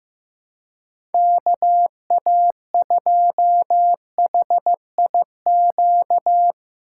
Morse code, heard as a keyed tone, about 15 wpm, 700 Hz. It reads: KA2HIQ
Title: USA Call Signs - 6 Characters - 2x Character Spacing - 15wpm